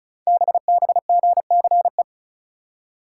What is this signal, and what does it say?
Morse code, heard as a keyed tone, about 33 wpm, 700 Hz. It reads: BBGCE
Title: Code Groups - 5 Letter:Numbers - 35wpm